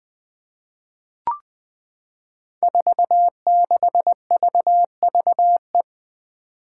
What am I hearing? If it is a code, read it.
Morse code, heard as a keyed tone, about 20 wpm, 700 Hz. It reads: 46VVE
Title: Five Challenging Letter-Number Combinations 20wpm